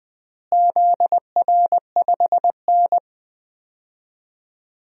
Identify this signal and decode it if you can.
Morse code, heard as a keyed tone, about 21 wpm, 700 Hz. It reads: ZR5N
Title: DXCC-100 Call Signs - 4 Characters - Repeated 3x - 20wpm